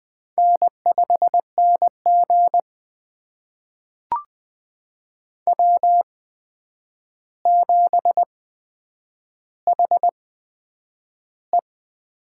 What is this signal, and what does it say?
Morse code, heard as a keyed tone, about 20 wpm, spaced as for 9 wpm, 700 Hz. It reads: N5NG W7HE
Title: USA Call Signs - 4 Characters - 8x Character Spacing - 20wpm